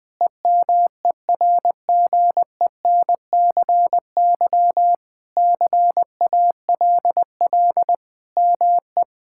Morse code, heard as a keyed tone, about 21 wpm, 700 Hz. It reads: EMERGENCY CALL ME
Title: Sentences with 3-Word Phrases - Spoken Beginning - 20wpm